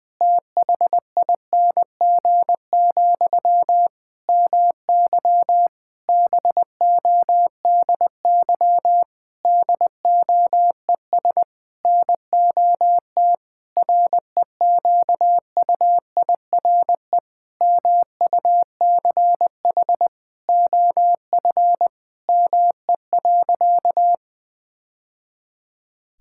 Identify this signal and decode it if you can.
Morse code, heard as a keyed tone, about 20 wpm, 700 Hz. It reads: THING, MY BODY DOES NOT REQUIRE MUCH OF ME.
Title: Sentences from Top 1000 Words - Review Part 6 - 20wpm